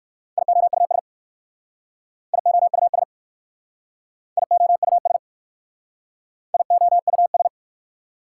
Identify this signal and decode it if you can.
Morse code, heard as a keyed tone, about 55 wpm, 700 Hz. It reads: IOUS IOUS IOUS IOUS
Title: Speed-Racing Prefixes and Suffixes 45wpm